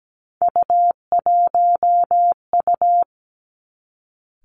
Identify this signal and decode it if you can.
Morse code, heard as a keyed tone, about 17 wpm, 700 Hz. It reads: U1U
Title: Three Challenging Letter-Number Combinations 17wpm